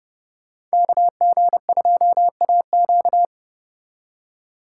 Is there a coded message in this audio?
Yes: Morse code from 0.7 to 3.2 s, heard as a keyed tone, about 34 wpm, 700 Hz. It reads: KG2AQ